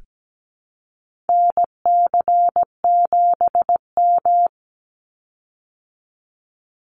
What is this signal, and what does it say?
Morse code, heard as a keyed tone, about 17 wpm, 700 Hz. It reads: NC7M
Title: Call Signs Repeated 3x - 17wpm